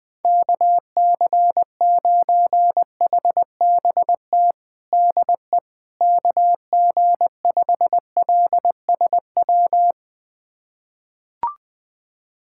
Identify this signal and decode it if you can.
Morse code, heard as a keyed tone, about 21 wpm, 700 Hz. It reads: KC9HBT DE KG5LSW
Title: USA Callsign DE Callsign 20wpm